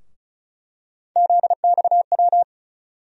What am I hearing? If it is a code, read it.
Morse code, heard as a keyed tone, about 36 wpm, 700 Hz. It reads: ZXW
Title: USA Call Sign Postfixes - 3 Letters 35wpm